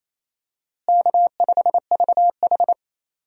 Morse code, heard as a keyed tone, about 27 wpm, 700 Hz. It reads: K5VH
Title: USA Call Signs - 4 Characters - Repeated 3x - 28wpm